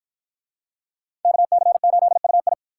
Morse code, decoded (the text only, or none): KC8UI